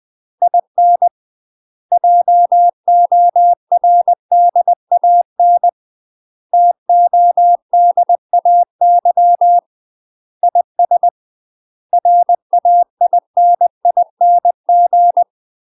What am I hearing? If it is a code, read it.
Morse code, heard as a keyed tone, about 20 wpm, 700 Hz. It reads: IN JORDAN TODAY IS RAINING
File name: UK MASTER QSOs TRAINING / 20WPM / UK MASTER QSOs 0037